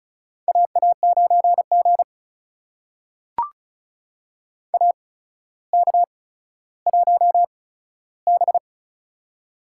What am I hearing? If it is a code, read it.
Morse code, heard as a keyed tone, about 36 wpm, spaced as for 15 wpm, 700 Hz. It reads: AA9G AK1B
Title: USA Call Signs - 4 Characters - 8x Character Spacing - 35wpm